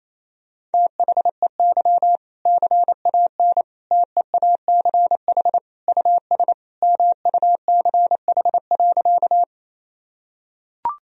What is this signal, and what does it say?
Morse code, heard as a keyed tone, about 28 wpm, 700 Hz. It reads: THEY CAN TEACH US MUCH.
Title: Sentences from Top 1000 Words - Review Part 2 - 28wpm